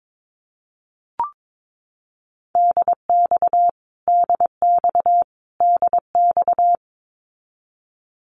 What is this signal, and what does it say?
Morse code, heard as a keyed tone, about 22 wpm, 700 Hz. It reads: DX DX DX
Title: Abbreviations Repeated 3x - 22wpm